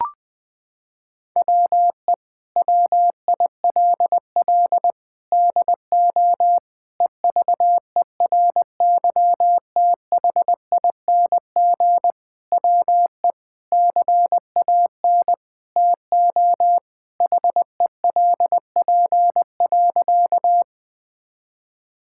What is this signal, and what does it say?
Morse code, heard as a keyed tone, about 21 wpm, 700 Hz. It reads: WE WILL DO EVERYTHING WE CAN TO HELP.
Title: Sentences with 6-Word Phrases - 20wpm